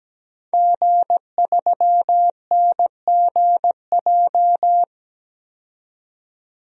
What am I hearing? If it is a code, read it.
Morse code, heard as a keyed tone, about 17 wpm, 700 Hz. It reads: G3NGJ
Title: DXCC-100 Call Signs - 5 Characters - 17wpm